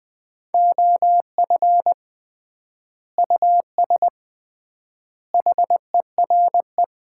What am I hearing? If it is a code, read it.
Morse code, heard as a keyed tone, about 21 wpm, 700 Hz. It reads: OF US HERE
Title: 3x Word-Spacing - Sets of 5 Words - 20wpm